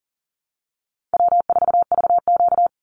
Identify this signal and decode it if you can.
Morse code, heard as a keyed tone, about 40 wpm, 700 Hz. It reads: W4VQ
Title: Easy Call Signs Repeated 3x - 40wpm